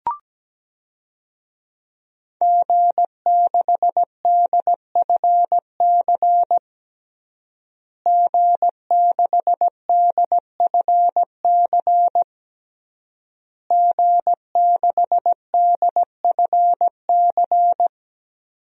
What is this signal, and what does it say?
Morse code, heard as a keyed tone, about 17 wpm, 700 Hz. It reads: G6DFC G6DFC G6DFC
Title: DXCC-100 Call Signs - 5 Characters - Repeated 3x - 17wpm